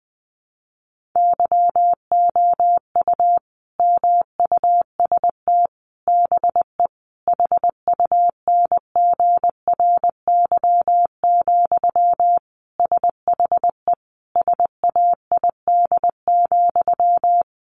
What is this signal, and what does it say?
Morse code, heard as a keyed tone, about 20 wpm, 700 Hz. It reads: YOU MUST BE HUNGRY, SHE SAID,
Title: Sentences from Top 1400 Words 20wpm